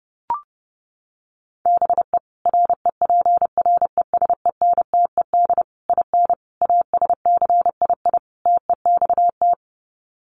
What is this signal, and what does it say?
Morse code, heard as a keyed tone, about 30 wpm, 700 Hz. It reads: BE REPRESENTED IN ASCII TEXT
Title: Sets of 5 Words 30wpm